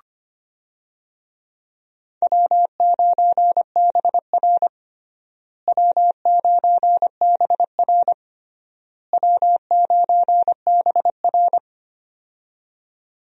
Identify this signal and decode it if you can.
Morse code, heard as a keyed tone, about 25 wpm, 700 Hz. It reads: W9BR W9BR W9BR
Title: USA Call Signs - 4 Characters - Repeated 3x - 25wpm